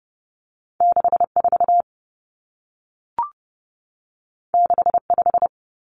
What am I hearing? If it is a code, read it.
Morse code, heard as a keyed tone, about 30 wpm, 700 Hz. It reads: 64 65
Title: Common Cut Numbers - 2-Digit 30wpm